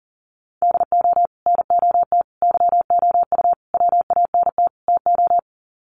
Morse code, heard as a keyed tone, about 40 wpm, 700 Hz. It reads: DO NOT YOU WANT TO
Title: Sets of 5 Words Encore2 40wpm